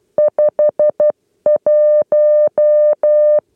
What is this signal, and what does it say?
Morse code, heard as a keyed tone, about 12 wpm, 600 Hz. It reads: 5 1
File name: 8 lezione cw 2